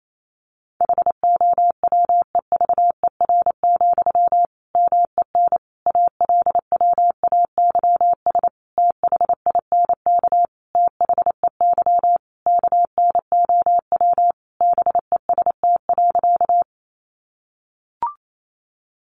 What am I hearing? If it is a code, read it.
Morse code, heard as a keyed tone, about 28 wpm, 700 Hz. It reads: HOWEVER, MEN ALWAYS THINK THEY KNOW BEST.
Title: Sentences from Top 600 Words 28wpm